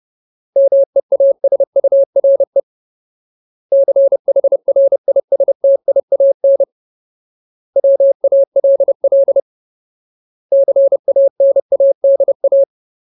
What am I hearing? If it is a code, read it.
Morse code, heard as a keyed tone, about 30 wpm, 550 Hz. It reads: MEASURE CHRISTIAN WALL CANADA
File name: Ultimate Word List / 30WPM Ultimate